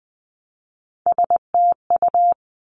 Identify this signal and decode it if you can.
Morse code, heard as a keyed tone, about 20 wpm, 700 Hz. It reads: STU
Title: Three Challenging Letter-Number Combinations 20wpm